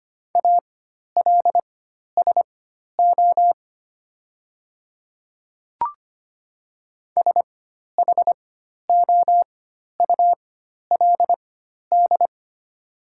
Morse code, heard as a keyed tone, about 25 wpm, spaced as for 12 wpm, 700 Hz. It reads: ALSO SHOULD
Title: 4x Character Spacing - Top 500 Words - 25wpm